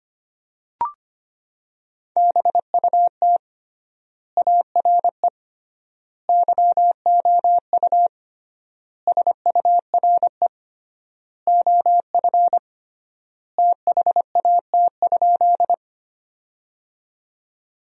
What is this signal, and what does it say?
Morse code, heard as a keyed tone, about 25 wpm, 700 Hz. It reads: BUT ARE YOU SURE OF THAT?
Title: 3x Word-Spacing - Sentences from Top 300 Words - 25wpm